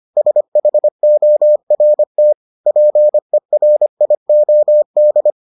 Morse code, heard as a keyed tone, about 25 wpm, 600 Hz. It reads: SHORT PERIOD